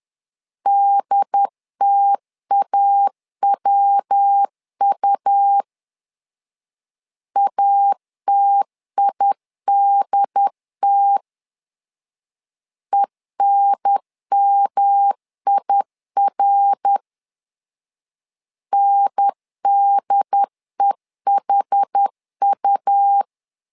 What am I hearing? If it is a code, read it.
Morse code, heard as a keyed tone, about 10 wpm, 800 Hz. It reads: DTAWU ATIDT ENMIR NDEHU
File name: LEZIONI / 4 Q 10wpm 2210